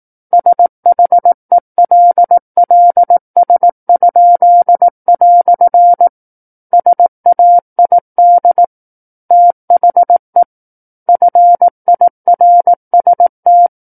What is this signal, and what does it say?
Morse code, heard as a keyed tone, about 18 wpm, 700 Hz. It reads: SHELLS?" SAID THE FIRST
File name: 18 WPM